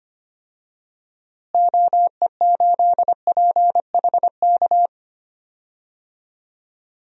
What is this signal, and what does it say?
Morse code, heard as a keyed tone, about 25 wpm, 700 Hz. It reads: OE8PHK
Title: DXCC-100 Call Signs - 6 Characters - Repeated 3x - 25wpm